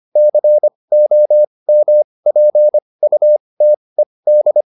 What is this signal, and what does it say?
Morse code, heard as a keyed tone, about 25 wpm, spaced as for 20 wpm, 600 Hz. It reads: COMPUTED
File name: ED8 20